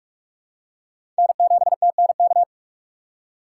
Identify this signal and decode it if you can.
Morse code, heard as a keyed tone, about 41 wpm, 700 Hz. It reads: N7TNK
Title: USA Call Signs - 6x Character Spacing - 45wpm